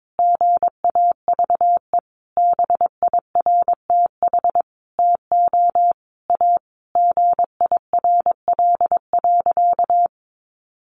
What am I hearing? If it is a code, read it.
Morse code, heard as a keyed tone, about 22 wpm, 700 Hz. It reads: GAVE BIRTH TO A GIRL.
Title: Sentences from Top 1100 Words 22wpm